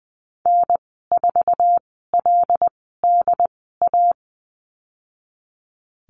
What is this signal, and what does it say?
Morse code, heard as a keyed tone, about 20 wpm, spaced as for 15 wpm, 700 Hz. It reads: N4LDA
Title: 186 Lesson 30g - Letter b - Call Signs 20wpm